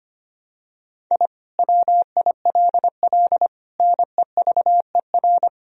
Morse code, heard as a keyed tone, about 25 wpm, 700 Hz. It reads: I WILL NEVER
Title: Sentences from Top 1000 Words - Review Part 1 - 25wpm